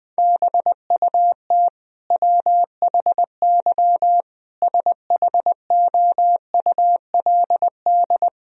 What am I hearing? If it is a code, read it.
Morse code, heard as a keyed tone, about 20 wpm, 700 Hz. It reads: BUT WHY SHOULD
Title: Sentences from Top 1000 Words - Review Part 1 - 20wpm